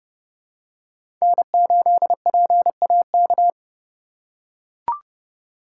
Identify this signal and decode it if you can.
Morse code, heard as a keyed tone, about 30 wpm, 700 Hz. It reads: N8PAK
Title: USA Call Signs - 5 Characters - 3x Character Spacing - 30wpm